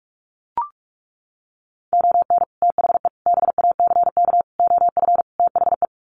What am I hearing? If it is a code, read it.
Morse code, heard as a keyed tone, about 44 wpm, 700 Hz. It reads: ON THE BACK OF THE
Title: Sets of 5 Words 45wpm